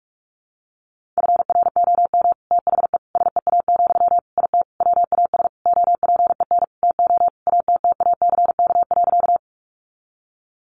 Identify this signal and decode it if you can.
Morse code, heard as a keyed tone, about 46 wpm, 700 Hz. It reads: FROM THE SEA, IT WAS OPEN TO ATTACK.
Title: Sentences from Top 1500 Words 45wpm